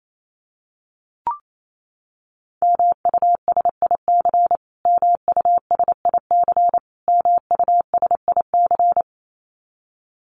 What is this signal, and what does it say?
Morse code, heard as a keyed tone, about 28 wpm, 700 Hz. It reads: MUSIC MUSIC MUSIC
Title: Top 500 Words Repeated 28wpm